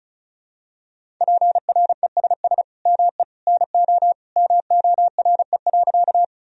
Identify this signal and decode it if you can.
Morse code, heard as a keyed tone, about 34 wpm, 700 Hz. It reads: PRESS ME NO MORE.
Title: Sentences from Top 1000 Words - Review Part 1 - 35wpm